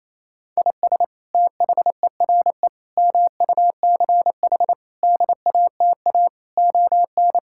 Morse code, heard as a keyed tone, about 27 wpm, 700 Hz. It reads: IS THERE MUCH DATA ON
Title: Sentences from Top 1000 Words - Review Part 6 - 28wpm